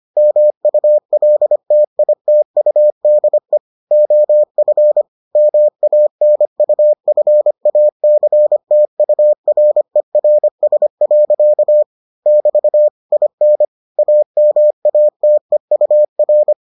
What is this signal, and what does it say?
Morse code, heard as a keyed tone, about 25 wpm, 600 Hz. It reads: MULTITUDE OF MANUFACTURERS. <BT> IN AMATEUR